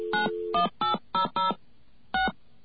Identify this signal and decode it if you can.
DTMF, heard as telephone touch tones, tones of about 120 ms. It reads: 91800 6
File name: off the hook